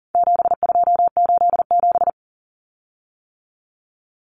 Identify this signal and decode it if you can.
Morse code, heard as a keyed tone, about 40 wpm, 700 Hz. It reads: 7287
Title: Common Cut Numbers - 4-Digit 40wpm